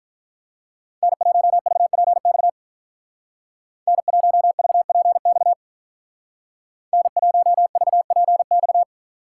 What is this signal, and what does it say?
Morse code, heard as a keyed tone, about 47 wpm, 700 Hz. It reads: N1VPX N1VPX N1VPX